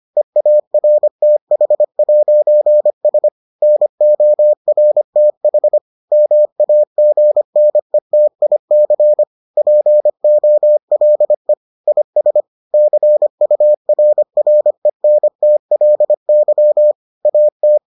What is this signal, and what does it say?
Morse code, heard as a keyed tone, about 25 wpm, 600 Hz. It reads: EARTH'S NORTH MAGNETIC POLE IS CURRENTLY AT